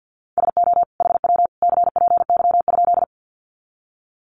Morse code, heard as a keyed tone, about 50 wpm, 700 Hz. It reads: SO HW CPY?
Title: Common QSO Phrases 50wpm